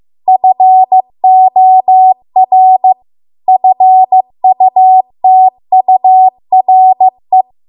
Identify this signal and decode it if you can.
Morse code, heard as a keyed tone, about 15 wpm, 750 Hz. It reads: FOR FUTURE